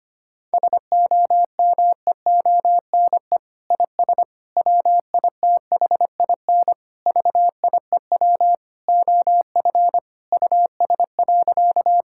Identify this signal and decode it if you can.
Morse code, heard as a keyed tone, about 25 wpm, 700 Hz. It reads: SOMEONE IS WITHIN VIEW OF US.
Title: Sentences from Top 1000 Words - Review Part 6 - 25wpm